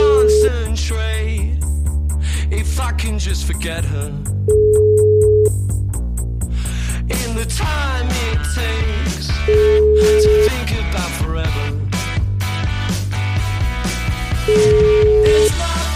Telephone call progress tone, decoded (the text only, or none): ringback tone